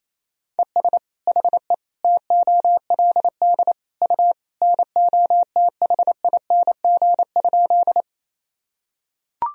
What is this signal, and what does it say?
Morse code, heard as a keyed tone, about 28 wpm, 700 Hz. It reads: ES HE TOLD U NOTHING?
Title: Sentences from Top 500 Words and Abbreviations 28wpm